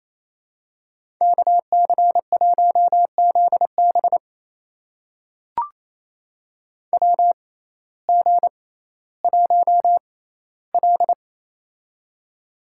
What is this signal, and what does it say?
Morse code, heard as a keyed tone, about 28 wpm, spaced as for 15 wpm, 700 Hz. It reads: KC1ZB WG1L
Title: USA Call Signs - 6x Character Spacing - 28wpm